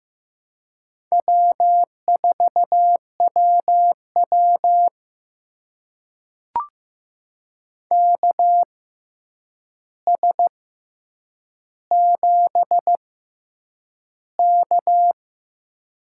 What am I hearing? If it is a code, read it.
Morse code, heard as a keyed tone, about 15 wpm, spaced as for 8 wpm, 700 Hz. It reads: W4WW KS7K
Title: USA Call Signs - 4 Characters - 6x Character Spacing - 15wpm